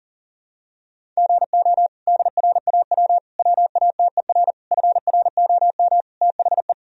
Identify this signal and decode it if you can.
Morse code, heard as a keyed tone, about 40 wpm, 700 Hz. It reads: GO DRAW WATER FROM THE
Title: Sentences from Top 1000 Words - Review Part 1 - 40wpm